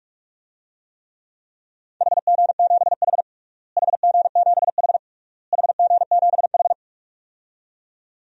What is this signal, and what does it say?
Morse code, heard as a keyed tone, about 48 wpm, 700 Hz. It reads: HG7H HG7H HG7H
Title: DXCC-100 Call Signs - 4 Characters - Repeated 3x - 45wpm